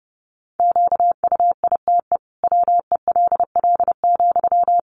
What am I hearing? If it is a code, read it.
Morse code, heard as a keyed tone, about 30 wpm, 700 Hz. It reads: QUITE WELL,
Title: Sentences from Top 700 Words 30wpm